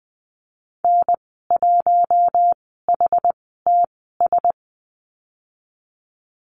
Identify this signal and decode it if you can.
Morse code, heard as a keyed tone, about 20 wpm, spaced as for 14 wpm, 700 Hz. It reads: N1HTS
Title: 154 Lesson 25g - Letter p - Call Signs 20wpm